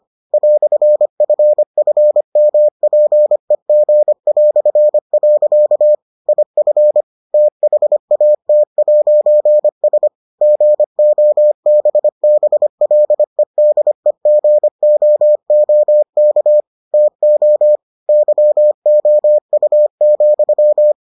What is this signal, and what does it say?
Morse code, heard as a keyed tone, about 25 wpm, 600 Hz. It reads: "FFMPEG". IF THAT'S GOBBLEDEGOOK TO YOU,